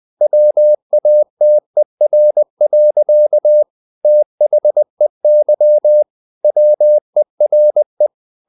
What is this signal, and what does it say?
Morse code, heard as a keyed tone, about 20 wpm, 600 Hz. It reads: WATER. THEY WERE
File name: the old man and the sea part 2 20wpm